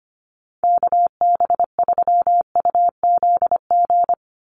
Morse code, heard as a keyed tone, about 25 wpm, 700 Hz. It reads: KB3UZG
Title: USA Call Signs 25wpm